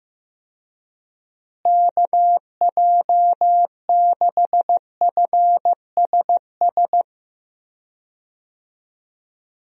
Morse code, heard as a keyed tone, about 15 wpm, 700 Hz. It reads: KJ6FSS
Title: USA Call Signs - 6 Characters - Repeated 2x - 15wpm